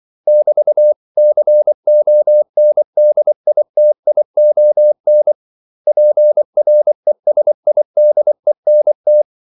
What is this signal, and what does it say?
Morse code, heard as a keyed tone, about 24 wpm, 600 Hz. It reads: <BT>CONDITION PRESIDENT
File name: CW / Practice / 500MostCommonEnglishWords24